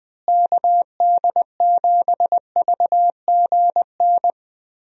Morse code, heard as a keyed tone, about 21 wpm, 700 Hz. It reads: KD7VGN
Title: USA Call Signs - 6 Characters - Repeated 2x - 20wpm